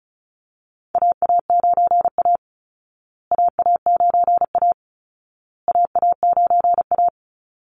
Morse code, heard as a keyed tone, about 34 wpm, 700 Hz. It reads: AA9A AA9A AA9A
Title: Call Signs Repeated 2x - 35wpm